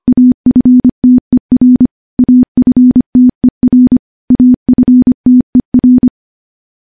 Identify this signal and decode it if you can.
Morse code, heard as a keyed tone, about 25 wpm, 250 Hz. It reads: AFTER AFTER AFTER
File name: morse-training-25WPM